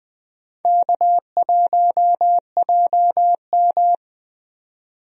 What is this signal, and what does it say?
Morse code, heard as a keyed tone, about 20 wpm, 700 Hz. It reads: K1JM
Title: USA Call Signs - 4 Characters - 20wpm